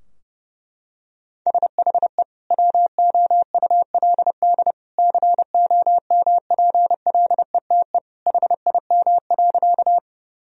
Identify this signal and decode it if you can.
Morse code, heard as a keyed tone, about 32 wpm, 700 Hz. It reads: SHE WOULD COMPLETE HIM.
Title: Sentences from Top 1000 Words - Review Part 2 - 30wpm